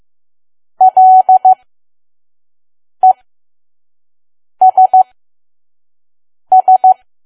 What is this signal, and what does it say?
Morse code, heard as a keyed tone, about 15 wpm, spaced as for 5 wpm, 750 Hz. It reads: LESS
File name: Archive / 5 WPM files / PM